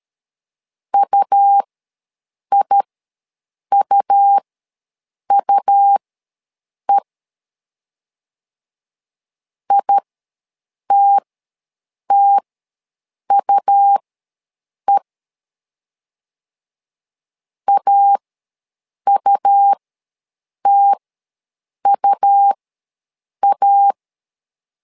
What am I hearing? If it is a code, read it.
Morse code, heard as a keyed tone, about 13 wpm, spaced as for 7 wpm, 800 Hz. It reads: UIUUE ITTUE AUTUA
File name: LEZIONI / 2 UHDN / 2 U ws 12wpm 1655